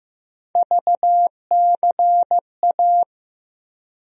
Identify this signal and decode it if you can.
Morse code, heard as a keyed tone, about 15 wpm, 700 Hz. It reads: VCA
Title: USA Call Sign Postfixes - 3 Letters 15wpm